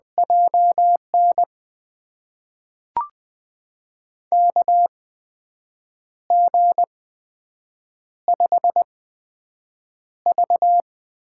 Morse code, heard as a keyed tone, about 21 wpm, spaced as for 7 wpm, 700 Hz. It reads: JN KG5V
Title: USA Call Signs - 6 Characters - 8x Character Spacing - 20wpm